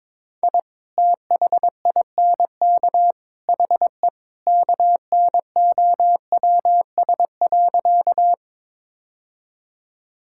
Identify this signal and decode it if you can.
Morse code, heard as a keyed tone, about 22 wpm, 700 Hz. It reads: I THINK HE KNOWS.
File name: practice / 22wpm / Sentences with 3-Word Phrases - 22wpm